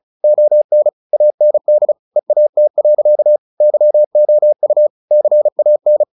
Morse code, heard as a keyed tone, about 34 wpm, 600 Hz. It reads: ON AND EAT. YOU CAN
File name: the old man and the sea / the old man and the sea 1 35wpm